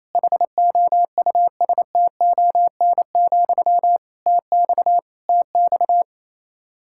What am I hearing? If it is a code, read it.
Morse code, heard as a keyed tone, about 28 wpm, 700 Hz. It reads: HOUSTON, TX TX
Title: K1USN SST QSO Practice 28wpm